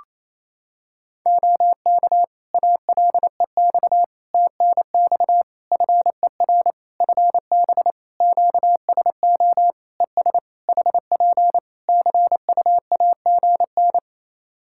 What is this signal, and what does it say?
Morse code, heard as a keyed tone, about 28 wpm, 700 Hz. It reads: OK ALEX TNX FER FB QSO ES HP CUAGN
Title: Practice QSO - Exchange 4 - Ending - 28wpm